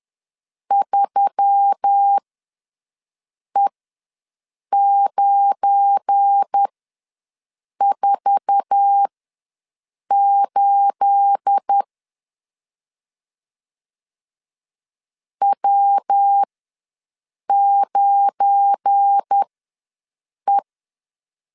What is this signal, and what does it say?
Morse code, heard as a keyed tone, about 10 wpm, spaced as for 6 wpm, 800 Hz. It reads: 3E948 W9E
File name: LEZIONI / 9 28 ws 10wpm 4124